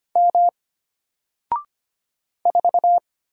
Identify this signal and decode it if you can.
Morse code, heard as a keyed tone, about 25 wpm, 700 Hz. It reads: M 4
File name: Single-Letter-Number-25wpm